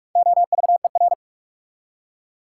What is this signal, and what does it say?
Morse code, heard as a keyed tone, about 58 wpm, 690 Hz. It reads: OVER